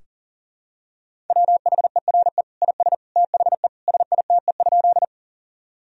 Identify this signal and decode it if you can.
Morse code, heard as a keyed tone, about 40 wpm, 700 Hz. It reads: WHERE IS THE SITE?
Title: Sentences from Top 1000 Words - Review Part 4 - 40wpm